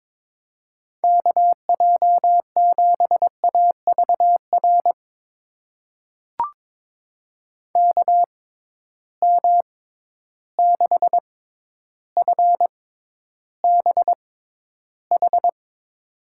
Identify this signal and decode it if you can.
Morse code, heard as a keyed tone, about 23 wpm, spaced as for 11 wpm, 700 Hz. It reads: KJ7AVR KM6FBH
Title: USA Call Signs - 6 Characters - 6x Character Spacing - 22wpm